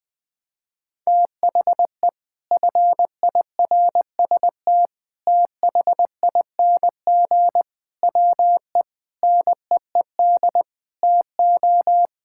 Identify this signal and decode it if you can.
Morse code, heard as a keyed tone, about 21 wpm, 700 Hz. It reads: THE FIRST THING WE NEED TO
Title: Sentences with 8-Word Phrases - Spoken Beginning - 20wpm